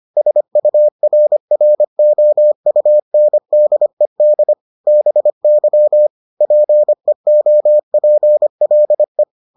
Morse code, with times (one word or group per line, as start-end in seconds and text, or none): 0.2-4.5 SURROUNDED
4.9-6.1 BY
6.4-9.2 PEOPLE